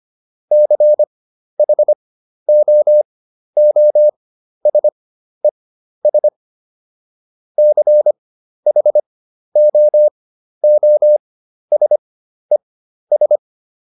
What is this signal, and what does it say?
Morse code, heard as a keyed tone, about 25 wpm, spaced as for 12 wpm, 600 Hz. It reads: CHOOSES CHOOSES